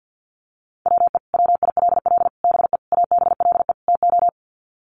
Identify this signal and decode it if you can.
Morse code, heard as a keyed tone, about 48 wpm, 700 Hz. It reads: WE WILL BE ABLE TO
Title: Sets of 5 Words Encore 50wpm